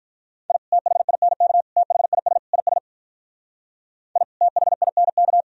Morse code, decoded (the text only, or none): I THINK THIS IS I THINK